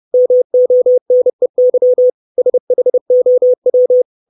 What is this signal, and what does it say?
Morse code, heard as a keyed tone, about 30 wpm, 500 Hz. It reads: MONEY SHOW
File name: 632 Words 30WPM